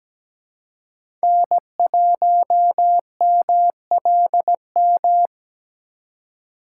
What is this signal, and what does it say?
Morse code, heard as a keyed tone, about 17 wpm, 700 Hz. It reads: N1MLM
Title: USA Call Signs - 4x Character Spacing - 17wpm